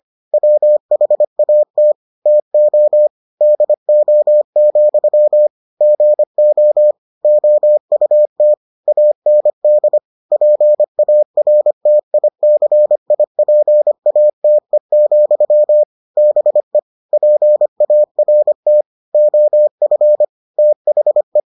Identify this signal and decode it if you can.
Morse code, heard as a keyed tone, about 25 wpm, 600 Hz. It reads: WHAT TO DO, GO OUT AND PARTICIPATE, BE PART OF THE